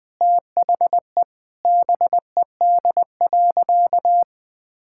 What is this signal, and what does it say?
Morse code, heard as a keyed tone, about 20 wpm, 700 Hz. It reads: THE BED.
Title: Sentences with 5-Word Phrases - 20wpm